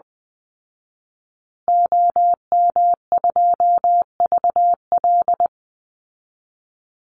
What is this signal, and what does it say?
Morse code, heard as a keyed tone, about 20 wpm, 700 Hz. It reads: OM2VL
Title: Call Signs Repeated 2x - 20wpm